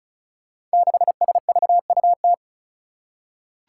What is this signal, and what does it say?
Morse code, heard as a keyed tone, about 36 wpm, 700 Hz. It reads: 6SVUT
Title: Five Challenging Letter-Number Combinations 35wpm